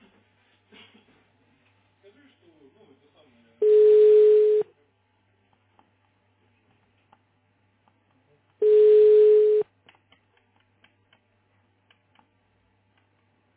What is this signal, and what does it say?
Ringback tone (Europe, 425 Hz), with speech.